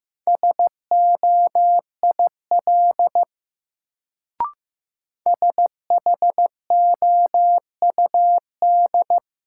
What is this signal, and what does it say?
Morse code, heard as a keyed tone, about 15 wpm, 700 Hz. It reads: SOIL SHOUD
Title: Sentences from Top 1000 Words - Review Part 4 - New Word Drill - 15wpm